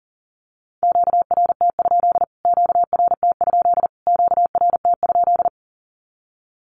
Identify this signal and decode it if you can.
Morse code, heard as a keyed tone, about 40 wpm, 700 Hz. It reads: QRT? QRT? QRT?
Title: Q-Codes Repeated 3x 40wpm